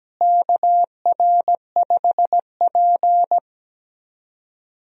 Morse code, heard as a keyed tone, about 17 wpm, 700 Hz. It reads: KR5P
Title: USA Call Signs - 4 Characters - Repeated 3x - 17wpm